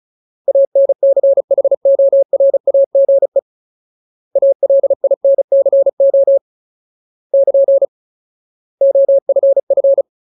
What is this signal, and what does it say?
Morse code, heard as a keyed tone, about 34 wpm, 550 Hz. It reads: ANCHORAGE ALINCO <KN> OFF
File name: Ultimate Word List / 35WPM Ultimate